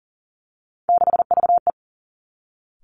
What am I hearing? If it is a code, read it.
Morse code, heard as a keyed tone, about 38 wpm, 700 Hz. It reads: 6VE
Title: Three Challenging Letter-Number Combinations 40wpm